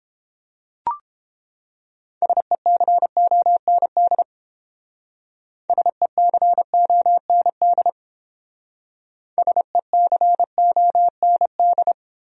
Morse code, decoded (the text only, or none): SECOND SECOND SECOND